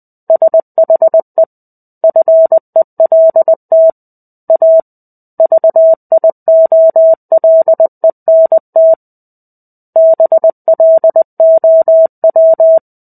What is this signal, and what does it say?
Morse code, heard as a keyed tone, about 20 wpm, 650 Hz. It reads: SHE FELT A VIOLENT BLOW
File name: 05-Chapter-05 20 WPM